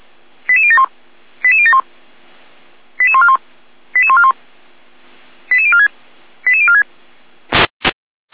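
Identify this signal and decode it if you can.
Five-tone selective call (ZVEI), heard as quick runs of five tones, tones of about 70 ms. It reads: 88071 88071 88141 88141 88046 88046
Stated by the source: ZVEI five-tone call 88012 received by SDR.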